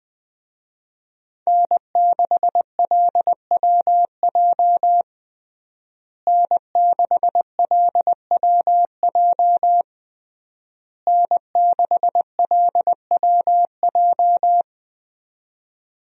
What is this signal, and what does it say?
Morse code, heard as a keyed tone, about 20 wpm, 700 Hz. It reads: N6LWJ N6LWJ N6LWJ
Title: USA Call Signs - 5 Characters - Repeated 3x - 20wpm